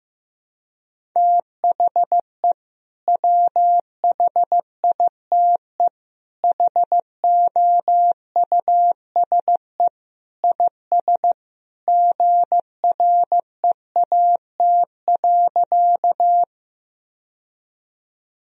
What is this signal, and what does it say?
Morse code, heard as a keyed tone, about 15 wpm, 700 Hz. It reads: THE WHITE HOUSE IS GREAT.
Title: Sentences from Top 1000 Words - Review Part 1 - 15wpm